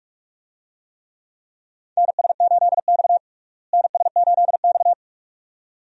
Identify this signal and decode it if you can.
Morse code, heard as a keyed tone, about 42 wpm, 700 Hz. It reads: NS8X NS8X
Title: USA Call Signs - 4 Characters - Repeated 2x - 45wpm